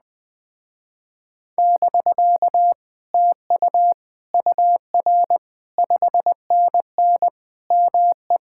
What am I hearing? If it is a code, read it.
Morse code, heard as a keyed tone, about 20 wpm, 700 Hz. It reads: <BK> TU UR 5NN ME